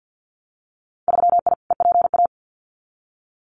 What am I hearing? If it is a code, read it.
Morse code, heard as a keyed tone, about 50 wpm, 700 Hz. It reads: 3I EPA
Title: Winter Field Day Exchanges Encore 50wpm